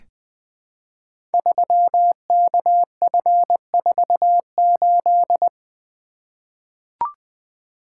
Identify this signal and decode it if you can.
Morse code, heard as a keyed tone, about 20 wpm, 700 Hz. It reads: 3KF48
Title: Code Groups - 5 Letter:Numbers - 20wpm